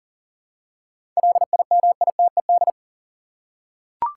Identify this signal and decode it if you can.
Morse code, heard as a keyed tone, about 40 wpm, 700 Hz. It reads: LIMITED
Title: Seven Letter Words - 40wpm